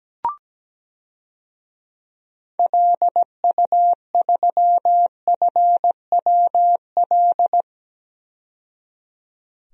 Morse code, heard as a keyed tone, about 17 wpm, 700 Hz. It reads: LU3FWL
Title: DXCC-100 Call Signs - 6 Characters - 17wpm